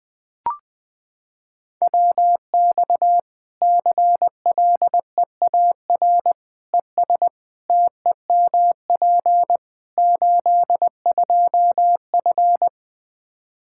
Morse code, heard as a keyed tone, about 20 wpm, 700 Hz. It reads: WX CLEAR ES TEMP 82F
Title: Practice QSO - Exchange 2 - Info - 20wpm